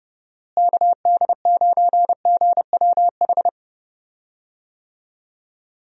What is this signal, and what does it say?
Morse code, heard as a keyed tone, about 30 wpm, 700 Hz. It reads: KD9GWH
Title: USA Call Signs - 6 Characters - Repeated 2x - 30wpm